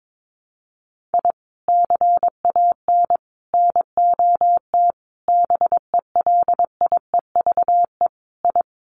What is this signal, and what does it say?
Morse code, heard as a keyed tone, about 22 wpm, 700 Hz. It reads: I CAN NOT BELIEVE I
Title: Sets of 5 Words Encore2 22wpm